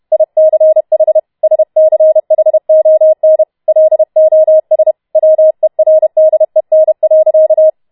Morse code, heard as a keyed tone, about 31 wpm, 630 Hz. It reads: ICH SCHON LOS WERDEN.